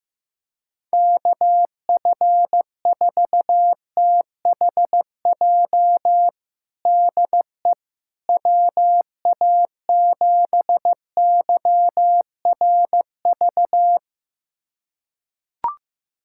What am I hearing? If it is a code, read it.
Morse code, heard as a keyed tone, about 15 wpm, 700 Hz. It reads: KF4THJ DE WA7YRV
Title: USA Callsign DE Callsign 15wpm